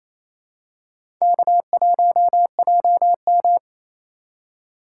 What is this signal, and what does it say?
Morse code, heard as a keyed tone, about 28 wpm, 700 Hz. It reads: K1JM
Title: USA Call Signs - 3x Character Spacing - 28wpm